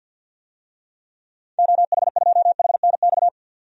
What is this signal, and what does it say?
Morse code, heard as a keyed tone, about 50 wpm, 700 Hz. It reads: OH2HNX
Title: DXCC-100 Call Signs - 6 Characters - 50wpm